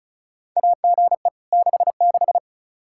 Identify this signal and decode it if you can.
Morse code, heard as a keyed tone, about 34 wpm, 700 Hz. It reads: AGE 66
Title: K1USN SST QSO Practice 35wpm